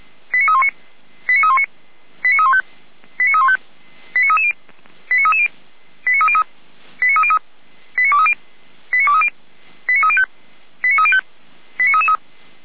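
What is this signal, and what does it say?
Five-tone selective call (ZVEI), heard as quick runs of five tones, tones of about 70 ms. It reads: 89319 89319 89315 89315 89330 89330 89303 89303 89230 89230 89305 89305 89333
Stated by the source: ZVEI five-tone selective call 89125 received on SDR.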